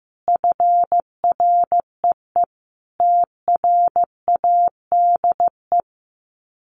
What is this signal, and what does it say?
Morse code, heard as a keyed tone, about 15 wpm, 700 Hz. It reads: FREE TRADE
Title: Sets of 5 Words Encore 15wpm